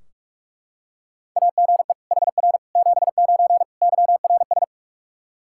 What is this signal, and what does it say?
Morse code, heard as a keyed tone, about 41 wpm, 700 Hz. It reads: AGE HR 79 YRS
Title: K1USN SST QSO Practice 45wpm